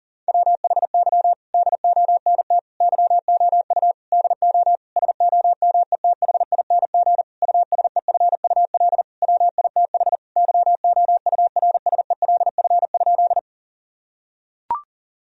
Morse code, heard as a keyed tone, about 40 wpm, 700 Hz. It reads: WHY DONT YOU DO SOMETHING USEFUL WITH YOURSELF?
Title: Sentences from Top 1000 Words - Review Part 4 - 40wpm